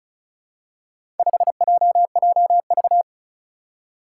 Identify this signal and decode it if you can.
Morse code, heard as a keyed tone, about 36 wpm, 700 Hz. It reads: 5JJV
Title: Four Challenging Letter-Number Combinations 35wpm